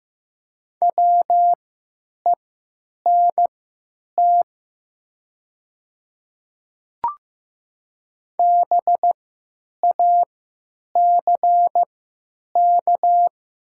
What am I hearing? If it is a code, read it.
Morse code, heard as a keyed tone, about 15 wpm, spaced as for 9 wpm, 700 Hz. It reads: WENT BACK
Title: 3x Character Spacing - Top 200 Words - 15wpm